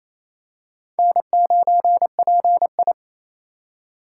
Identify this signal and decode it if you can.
Morse code, heard as a keyed tone, about 28 wpm, 700 Hz. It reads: N9PI
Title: USA Call Signs - 4 Characters - 3x Character Spacing - 28wpm